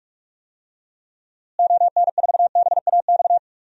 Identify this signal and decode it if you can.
Morse code, heard as a keyed tone, about 41 wpm, 700 Hz. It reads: ON4BAX
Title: DXCC-100 Call Signs - 6 Characters - Repeated 2x - 45wpm